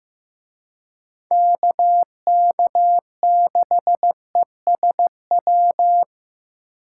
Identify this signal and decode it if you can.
Morse code, heard as a keyed tone, about 18 wpm, 700 Hz. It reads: KK6ESW
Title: Speed Racing - USA 2x3 Call Signs 15wpm